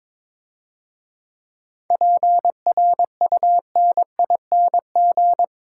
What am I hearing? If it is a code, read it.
Morse code, heard as a keyed tone, about 22 wpm, 690 Hz. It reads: PRUNING
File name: Easy Words - Speed-Racing with Hints - 22wpm